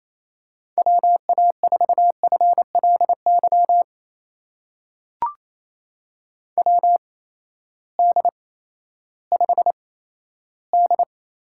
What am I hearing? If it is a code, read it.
Morse code, heard as a keyed tone, about 28 wpm, spaced as for 14 wpm, 700 Hz. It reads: WA4FLY WD5D
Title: USA Call Signs - 6 Characters - 8x Character Spacing - 28wpm